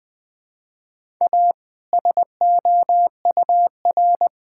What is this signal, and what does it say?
Morse code, heard as a keyed tone, about 20 wpm, 700 Hz. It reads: A SOUR
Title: All-Age Appropriate Jokes - 20wpm